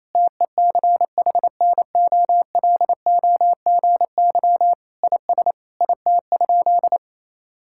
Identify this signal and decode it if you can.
Morse code, heard as a keyed tone, about 28 wpm, 700 Hz. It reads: TECHNOLOGY IS IT?
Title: Sentences from Top 1000 Words - Review Part 5 - 28wpm